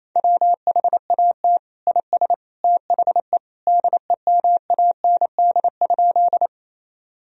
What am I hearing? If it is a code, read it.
Morse code, heard as a keyed tone, about 28 wpm, 700 Hz. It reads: WHAT IS THE DEMAND?
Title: Sentences from Top 1000 Words - Review Part 4 - 28wpm